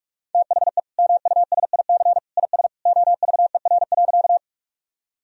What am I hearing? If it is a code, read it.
Morse code, heard as a keyed tone, about 43 wpm, 700 Hz. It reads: THE MUSIC IS OVER.
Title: Sentences from Top 1000 Words - Review Part 1 - 45wpm